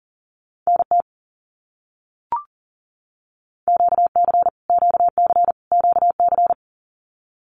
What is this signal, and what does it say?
Morse code, heard as a keyed tone, about 40 wpm, 700 Hz. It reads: NT QC QC QC
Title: Canadian Provinces and Territories - Repeated 3x - 40wpm